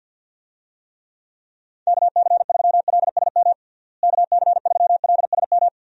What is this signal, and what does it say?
Morse code, heard as a keyed tone, about 50 wpm, 700 Hz. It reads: KC3LSM KC3LSM
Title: USA Call Signs - 6 Characters - Repeated 2x - 50wpm